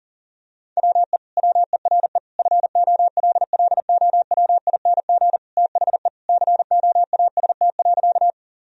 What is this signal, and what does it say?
Morse code, heard as a keyed tone, about 40 wpm, 700 Hz. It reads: WE WERE FOLLOWING THE COAST.
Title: Sentences from Top 1000 Words - Review Part 4 - 40wpm